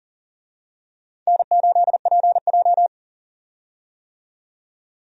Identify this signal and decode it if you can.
Morse code, heard as a keyed tone, about 40 wpm, 700 Hz. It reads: N8PJ
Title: USA Call Signs - 4 Characters - 40wpm